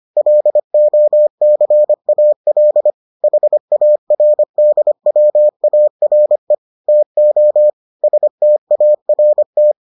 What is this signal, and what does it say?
Morse code, heard as a keyed tone, about 25 wpm, 600 Hz. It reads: LOCAL HARDWARE TO START